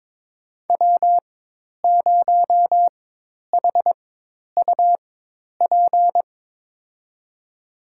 Morse code, heard as a keyed tone, about 22 wpm, spaced as for 10 wpm, 700 Hz. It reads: W0HUP
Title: USA Call Signs - 5 Characters - 4x Character Spacing - 22wpm